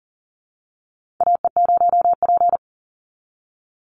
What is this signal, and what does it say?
Morse code, heard as a keyed tone, about 41 wpm, 700 Hz. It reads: AE0P
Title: Call Signs - 40wpm